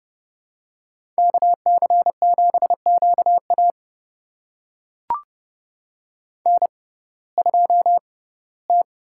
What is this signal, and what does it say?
Morse code, heard as a keyed tone, about 32 wpm, spaced as for 19 wpm, 700 Hz. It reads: KC7QA N2T
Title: USA Call Signs - 6x Character Spacing - 30wpm